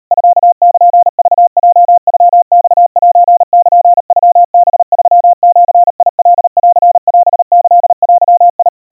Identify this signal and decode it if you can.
Morse code (Japanese Wabun code), heard as a keyed tone, about 38 wpm, 700 Hz. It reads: ミルクヲノマセルノハ3ジカンオキデ
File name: kawa / Morse sample / kamyu 140 cpm 700 Hz Crispy